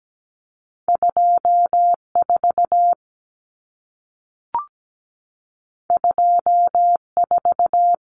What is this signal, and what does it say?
Morse code, heard as a keyed tone, about 17 wpm, 700 Hz. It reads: 24 24
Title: Numbers - 2-Digit 17wpm